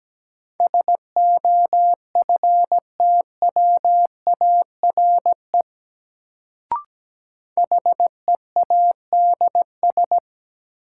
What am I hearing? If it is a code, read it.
Morse code, heard as a keyed tone, about 17 wpm, 700 Hz. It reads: SOFTWARE HEADS
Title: Sentences from Top 1000 Words - Review Part 3 - New Word Drill - 17wpm